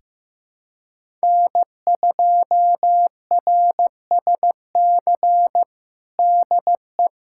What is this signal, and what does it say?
Morse code, heard as a keyed tone, about 15 wpm, 700 Hz. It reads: N2RSC DE
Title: USA Callsign DE Callsign - 5 Characters - 15wpm